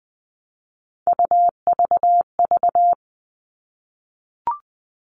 Morse code, heard as a keyed tone, about 20 wpm, 700 Hz. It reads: UVV
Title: All Cut Numbers - 3-Digit 20wpm